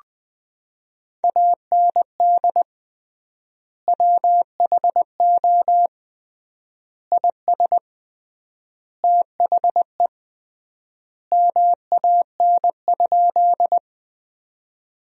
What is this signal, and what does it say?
Morse code, heard as a keyed tone, about 21 wpm, 700 Hz. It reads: AND WHO IS THE MAN?
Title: 3x Word-Spacing - Sentences from Top 200 Words - 20wpm